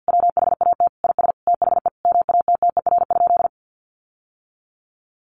Morse code, heard as a keyed tone, about 48 wpm, 700 Hz. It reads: WHAT IS THE MATTER?
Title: Sentences from Top 700 Words 50wpm